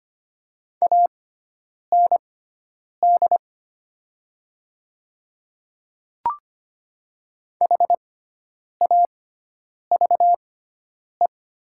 Morse code, heard as a keyed tone, about 25 wpm, spaced as for 9 wpm, 700 Hz. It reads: AND HAVE
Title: 6x Character Spacing - Top 100 Words - 25wpm